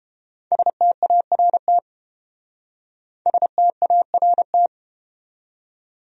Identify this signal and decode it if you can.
Morse code, heard as a keyed tone, about 31 wpm, 700 Hz. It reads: START START